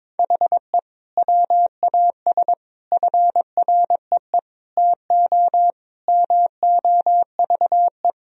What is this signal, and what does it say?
Morse code, heard as a keyed tone, about 21 wpm, 700 Hz. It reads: HE WAS FREE TO MOVE
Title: Sentences from Top 1000 Words - Review Part 1 - 22wpm